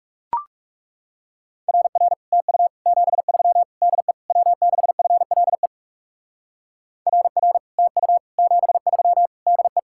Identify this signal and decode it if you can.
Morse code, heard as a keyed tone, about 40 wpm, 700 Hz. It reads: RR TU 73 DE W6FLE RR TU 73 DE